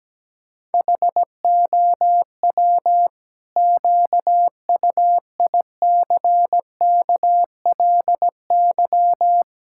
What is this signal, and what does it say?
Morse code, heard as a keyed tone, about 17 wpm, 700 Hz. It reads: HOW QUICKLY
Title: Pangram Challenge v2 - 17wpm